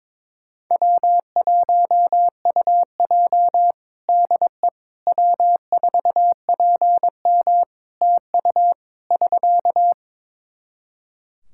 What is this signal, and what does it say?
Morse code, heard as a keyed tone, about 22 wpm, 700 Hz. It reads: W1UJ DE W4PM TU <SK>
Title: K1USN SST QSO Practice 22wpm